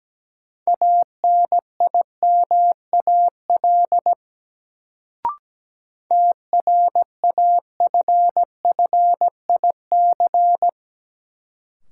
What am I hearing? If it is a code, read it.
Morse code, heard as a keyed tone, about 17 wpm, 700 Hz. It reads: ANIMAL TRAFFIC
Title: Sentences from Top 1300 Words - New Word Drill Alt - 17wpm